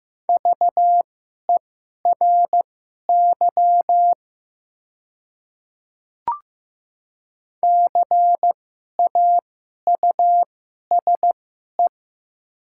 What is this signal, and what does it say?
Morse code, heard as a keyed tone, about 15 wpm, spaced as for 11 wpm, 700 Hz. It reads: VERY CAUSE
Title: 2x Character Spacing - Top 300 Words - 15wpm